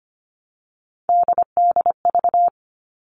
Morse code, heard as a keyed tone, about 25 wpm, 700 Hz. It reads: DDV
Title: All Cut Numbers - 3-Digit 25wpm